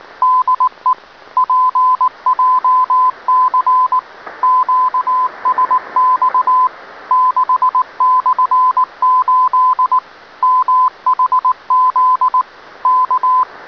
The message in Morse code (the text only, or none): DE PJC QSX 6/8 MHZ K